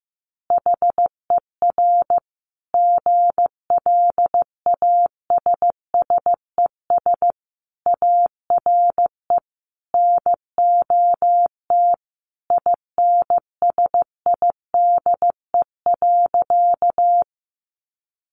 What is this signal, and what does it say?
Morse code, heard as a keyed tone, about 15 wpm, 700 Hz. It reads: HER GLASSES ARE NOT INSIDE.
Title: Additional Easy Phrases 15wpm